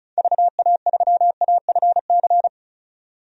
Morse code, heard as a keyed tone, about 34 wpm, 700 Hz. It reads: VA3AFC
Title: DXCC-100 Call Signs - 6 Characters - 35wpm